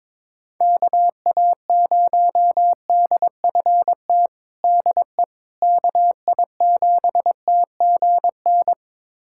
Morse code, heard as a keyed tone, about 21 wpm, 700 Hz. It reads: KA0DFT DE KI7TGN
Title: USA Callsign DE Callsign - 6 Characters - 22wpm